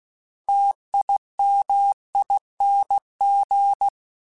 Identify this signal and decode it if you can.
Morse code, heard as a keyed tone, about 16 wpm, 800 Hz. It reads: TIMING
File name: Special INGs / Special INGs-mp3s / Sngl ing @15